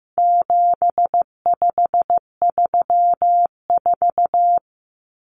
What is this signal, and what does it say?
Morse code, heard as a keyed tone, about 15 wpm, 700 Hz. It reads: 7534
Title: Common Cut Numbers - 4-Digit 15wpm